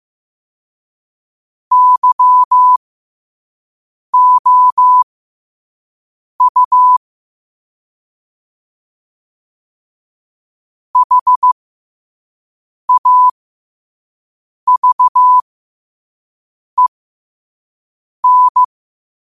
Morse code, heard as a keyed tone, about 15 wpm, spaced as for 5 wpm, 1000 Hz. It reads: YOU HAVEN